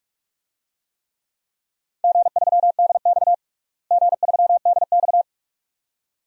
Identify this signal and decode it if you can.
Morse code, heard as a keyed tone, about 46 wpm, 700 Hz. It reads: G3DX G3DX
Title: DXCC-100 Call Signs - 4 Characters - Repeated 2x - 45wpm